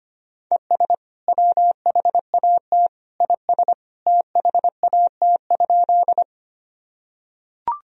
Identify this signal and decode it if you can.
Morse code, heard as a keyed tone, about 25 wpm, 700 Hz. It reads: ES WHAT IS THAT?
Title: Sentences from Top 100 Words and Abbreviations 25wpm